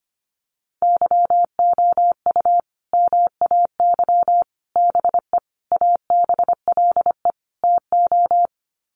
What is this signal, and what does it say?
Morse code, heard as a keyed tone, about 25 wpm, 700 Hz. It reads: YOU MAY BE ABLE TO
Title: Sets of 5 Words Encore 25wpm